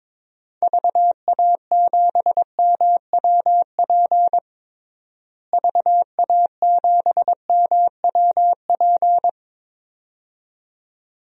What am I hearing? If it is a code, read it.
Morse code, heard as a keyed tone, about 22 wpm, 700 Hz. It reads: VA7MWP VA7MWP
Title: DXCC-100 Call Signs - 6 Characters - Repeated 2x - 22wpm